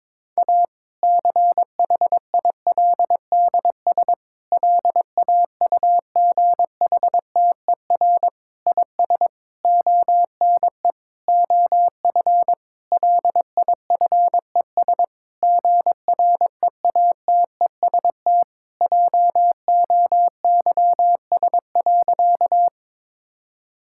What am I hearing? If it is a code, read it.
Morse code, heard as a keyed tone, about 23 wpm, 700 Hz. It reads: A CHILDS LAUGHTER IS ONE OF LIFES GREATEST JOYS.
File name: Difficult Sentences with Hints - 22wpm